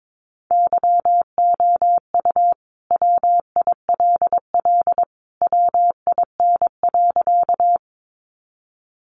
Morse code, heard as a keyed tone, about 22 wpm, 700 Hz. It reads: YOU WILL WIN.
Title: Sentences from Top 1500 Words 22wpm